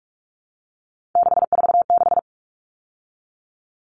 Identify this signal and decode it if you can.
Morse code, heard as a keyed tone, about 44 wpm, 700 Hz. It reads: 646
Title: Numbers - 3-Digit 45wpm